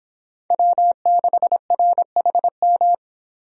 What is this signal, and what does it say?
Morse code, heard as a keyed tone, about 25 wpm, 700 Hz. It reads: W6RHM